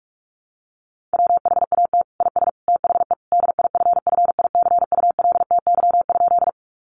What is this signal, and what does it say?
Morse code, heard as a keyed tone, about 44 wpm, 700 Hz. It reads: WHAT IS THE DIFFICULTY?
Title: Sentences from Top 1500 Words 45wpm